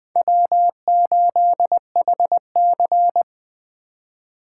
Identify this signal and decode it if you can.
Morse code, heard as a keyed tone, about 20 wpm, 700 Hz. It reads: W8HC
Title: USA Call Signs - 2x Character Spacing - 20wpm